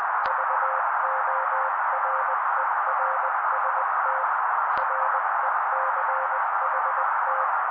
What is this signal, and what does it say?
Morse code, heard as a keyed tone, about 20 wpm, 530 Hz. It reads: VORERST RECHT
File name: daily / 18wpm noise